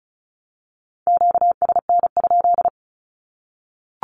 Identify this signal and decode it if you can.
Morse code, heard as a keyed tone, about 34 wpm, 700 Hz. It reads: QSN?
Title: Q-Codes 35wpm